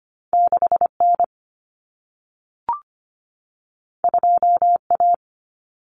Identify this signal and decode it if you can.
Morse code, heard as a keyed tone, about 25 wpm, 700 Hz. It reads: 6N 2A
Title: Common Cut Numbers - 2-Digit 25wpm